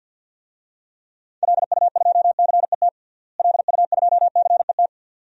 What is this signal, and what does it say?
Morse code, heard as a keyed tone, about 47 wpm, 700 Hz. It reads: LU2CET LU2CET
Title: DXCC-100 Call Signs - 6 Characters - Repeated 2x - 50wpm